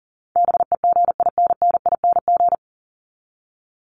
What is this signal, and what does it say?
Morse code, heard as a keyed tone, about 40 wpm, 700 Hz. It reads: BEGINNING
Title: ING Suffix - Top 100 Words with Base Words 40wpm